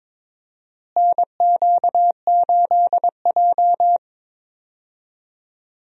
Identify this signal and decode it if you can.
Morse code, heard as a keyed tone, about 22 wpm, 700 Hz. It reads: NQ8J
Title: USA Call Signs - 4 Characters - Repeated 2x - 22wpm